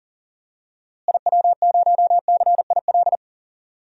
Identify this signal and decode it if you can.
Morse code, heard as a keyed tone, about 40 wpm, 700 Hz. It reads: IW0CIL
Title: DXCC-100 Call Signs - 6 Characters - Repeated 2x - 40wpm